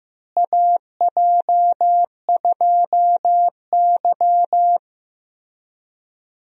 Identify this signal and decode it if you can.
Morse code, heard as a keyed tone, about 15 wpm, 700 Hz. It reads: AJ2Y
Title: USA Call Signs - 4 Characters - Repeated 3x - 15wpm